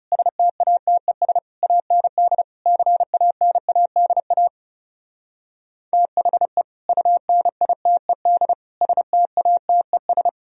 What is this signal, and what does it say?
Morse code, heard as a keyed tone, about 39 wpm, 700 Hz. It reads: STATES AND CANADA THE UNITED STATES